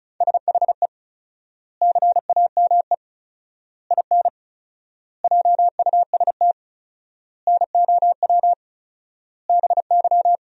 Morse code, heard as a keyed tone, about 34 wpm, 700 Hz. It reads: SHE CAME IN JUST NOW BY
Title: 4x Word-Spacing - Sentences from Top 300 Words - 35wpm